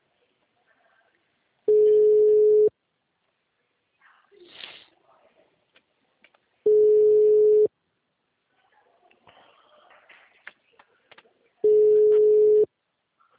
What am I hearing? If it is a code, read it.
Ringback tone (Europe, 425 Hz).